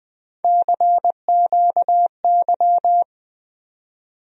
Morse code, heard as a keyed tone, about 20 wpm, 700 Hz. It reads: CQY
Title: USA Call Sign Postfixes - 3 Letters 20wpm